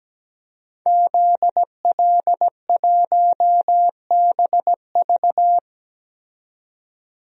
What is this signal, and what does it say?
Morse code, heard as a keyed tone, about 17 wpm, 700 Hz. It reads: ZL1BV
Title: DXCC-100 Call Signs - 5 Characters - 17wpm